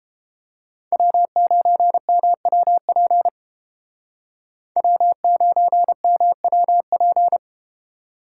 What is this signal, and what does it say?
Morse code, heard as a keyed tone, about 32 wpm, 700 Hz. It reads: W9MWP W9MWP